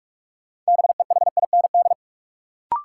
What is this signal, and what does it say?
Morse code, heard as a keyed tone, about 45 wpm, 700 Hz. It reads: BEHIND
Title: Sentences from Top 500 Words - New Word Drill - 45wpm